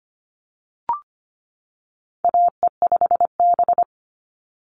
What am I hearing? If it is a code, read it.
Morse code, heard as a keyed tone, about 25 wpm, 700 Hz. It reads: AE5B
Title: NA SOTA Chaser Callsigns 25wpm